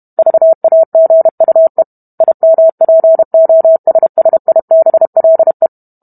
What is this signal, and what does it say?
Morse code, heard as a keyed tone, about 32 wpm, 650 Hz. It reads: VAGUE IMPOSSIBLE